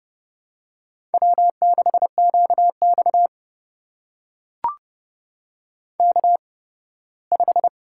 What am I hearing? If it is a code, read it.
Morse code, heard as a keyed tone, about 31 wpm, spaced as for 18 wpm, 700 Hz. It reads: W6QX K5
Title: USA Call Signs - 8x Character Spacing - 30wpm